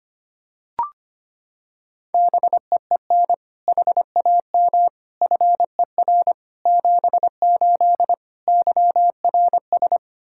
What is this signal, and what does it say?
Morse code, heard as a keyed tone, about 25 wpm, 700 Hz. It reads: BEEN HAM FER 78 YRS
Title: K1USN SST QSO Practice 25wpm